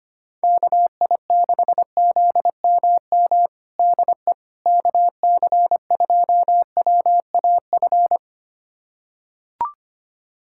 Morse code, heard as a keyed tone, about 25 wpm, 700 Hz. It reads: KI6ZMM DE KC2WAF
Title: USA Callsign DE Callsign 25wpm